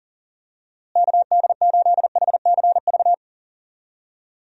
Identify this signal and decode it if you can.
Morse code, heard as a keyed tone, about 40 wpm, 700 Hz. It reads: KD8HCV
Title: USA Call Signs - 6 Characters - Repeated 2x - 40wpm